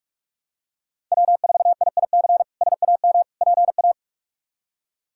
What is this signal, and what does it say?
Morse code, heard as a keyed tone, about 45 wpm, 700 Hz. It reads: W4IIC SAM PA
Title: K1USN SST QSO Practice - 45wpm